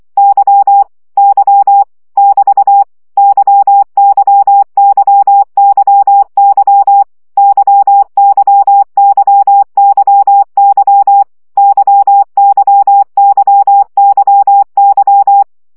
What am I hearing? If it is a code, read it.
Morse code, heard as a keyed tone, about 24 wpm, 800 Hz. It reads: Y Y <BT> YYYYY YYYYY YYYYY
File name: CW gyakorlas y 20 baud